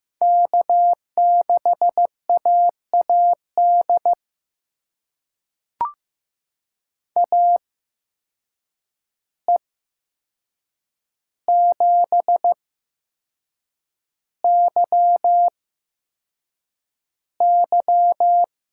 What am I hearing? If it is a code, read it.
Morse code, heard as a keyed tone, about 15 wpm, spaced as for 6 wpm, 700 Hz. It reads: K6AAD AE7YY
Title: USA Call Signs - 5 Characters - 8x Character Spacing - 15wpm